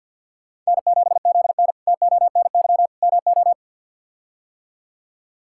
Morse code, heard as a keyed tone, about 52 wpm, 700 Hz. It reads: N7ZN TONY MO
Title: K1USN SST QSO Practice - 50wpm